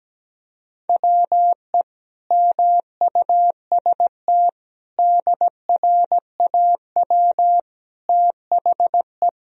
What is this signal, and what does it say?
Morse code, heard as a keyed tone, about 17 wpm, 700 Hz. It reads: WE MUST DRAW THE
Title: Sentences from Top 1000 Words - Review Part 1 - 17wpm